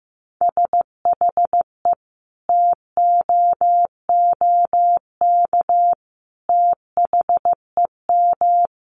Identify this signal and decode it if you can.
Morse code, heard as a keyed tone, about 15 wpm, 700 Hz. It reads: SHE TOOK THEM
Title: Sentences from Top 1300 Words 15wpm